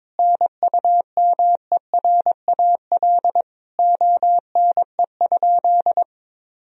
Morse code, heard as a keyed tone, about 22 wpm, 700 Hz. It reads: NUMERAL ONE?
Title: Sentences from Top 1000 Words - Review Part 2 - 22wpm